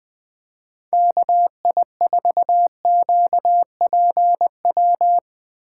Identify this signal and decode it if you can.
Morse code, heard as a keyed tone, about 19 wpm, 700 Hz. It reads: KI4QPW